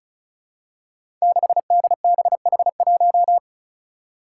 Morse code, heard as a keyed tone, about 36 wpm, 700 Hz. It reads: 6DBH1
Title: Five Challenging Letter-Number Combinations 35wpm